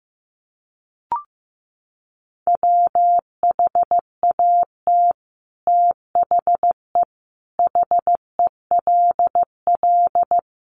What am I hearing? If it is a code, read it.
Morse code, heard as a keyed tone, about 15 wpm, 700 Hz. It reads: WHAT THE HELL
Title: Sets of 5 Words Encore 15wpm